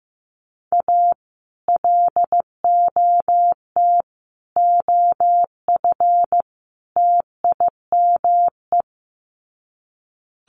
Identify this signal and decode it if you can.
Morse code, heard as a keyed tone, about 15 wpm, 700 Hz. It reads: A LOT OF TIME
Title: Sets of 4 Words 15wpm